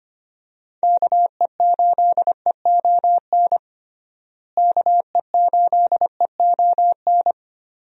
Morse code, heard as a keyed tone, about 25 wpm, 700 Hz. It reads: KE8EON KE8EON
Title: USA Call Signs - 6 Characters - Repeated 2x - 25wpm